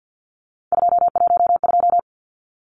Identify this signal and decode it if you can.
Morse code, heard as a keyed tone, about 55 wpm, 700 Hz. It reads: 212
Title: Numbers - 3-Digit 50wpm